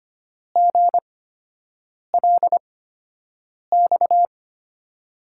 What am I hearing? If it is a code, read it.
Morse code, heard as a keyed tone, about 25 wpm, spaced as for 7 wpm, 700 Hz. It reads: GLX
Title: USA Call Signs - 5 Characters - 8x Character Spacing - 25wpm